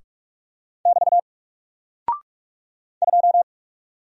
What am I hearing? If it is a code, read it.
Morse code, heard as a keyed tone, about 44 wpm, 700 Hz. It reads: <BT> 2
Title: Single Character (Letter:Number:Punctuation:Prosign) 45wpm